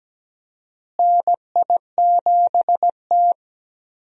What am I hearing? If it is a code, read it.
Morse code, heard as a keyed tone, about 17 wpm, 700 Hz. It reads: NI7T
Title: USA Call Signs - 4 Characters - Repeated 3x - 17wpm